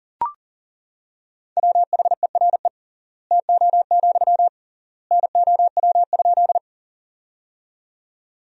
Morse code, heard as a keyed tone, about 40 wpm, 700 Hz. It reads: WHERE TO, NOW?
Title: 3x Word-Spacing - Sentences from Top 200 Words - 40wpm